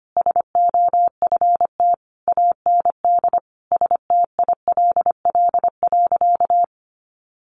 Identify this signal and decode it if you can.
Morse code, heard as a keyed tone, about 25 wpm, 700 Hz. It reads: SOFT AND STILL.
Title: Sentences from Top 1500 Words 25wpm